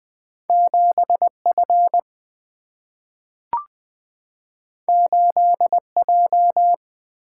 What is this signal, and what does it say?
Morse code, heard as a keyed tone, about 20 wpm, 700 Hz. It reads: 7F 8J
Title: Single-Letter Single-Number 20wpm